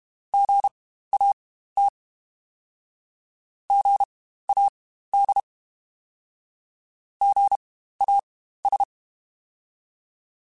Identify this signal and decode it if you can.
Morse code, heard as a keyed tone, about 32 wpm, spaced as for 15 wpm, 800 Hz. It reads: GAT GAD GAS
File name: Spaced Alphas / Spaced Alphas mp3s / Gs sp @30